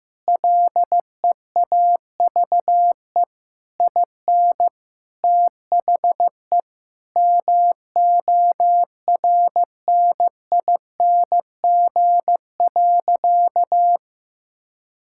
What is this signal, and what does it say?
Morse code, heard as a keyed tone, about 15 wpm, 700 Hz. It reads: LEAVE IN THE MORNING.
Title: Sentences from Top 1000 Words - Review Part 6 - 15wpm